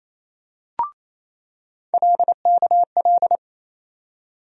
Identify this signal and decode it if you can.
Morse code, heard as a keyed tone, about 28 wpm, 700 Hz. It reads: LKL
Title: USA Call Sign Postfixes - 3 Letters 28wpm